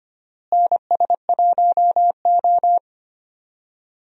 Morse code, heard as a keyed tone, about 25 wpm, 700 Hz. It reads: NS1O
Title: USA Call Signs - 4 Characters - Repeated 3x - 25wpm